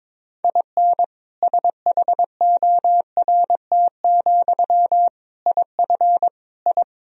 Morse code, heard as a keyed tone, about 21 wpm, 700 Hz. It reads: IN SHORT, IF I
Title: Sentences from Top 1000 Words - Review Part 2 - 22wpm